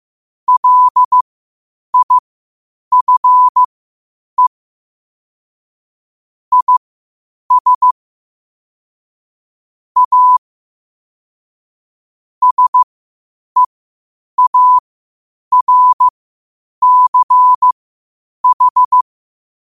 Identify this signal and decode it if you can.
Morse code, heard as a keyed tone, about 15 wpm, spaced as for 8 wpm, 1000 Hz. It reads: LIFE IS A SEARCH
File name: qotdfiles / 8wpm